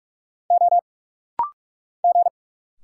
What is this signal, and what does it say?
Morse code, heard as a keyed tone, about 44 wpm, 700 Hz. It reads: O G
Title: Single Letters - Rapid-Fire 45wpm